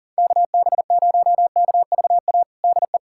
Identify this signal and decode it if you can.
Morse code, heard as a keyed tone, about 37 wpm, 700 Hz. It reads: KB0KVA DE
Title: USA Callsign DE Callsign - 6 Characters - 40wpm